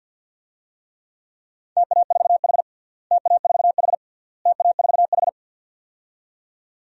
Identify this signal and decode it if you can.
Morse code, heard as a keyed tone, about 52 wpm, 700 Hz. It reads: TA4H TA4H TA4H
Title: DXCC-100 Call Signs - 4 Characters - Repeated 3x - 50wpm